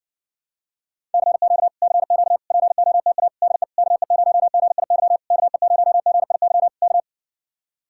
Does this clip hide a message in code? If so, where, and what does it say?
Yes: Morse code from 1.1 to 7.0 s, heard as a keyed tone, about 60 wpm, 700 Hz. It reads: CQ CQ POTA DE KE0GIY KE0GIY K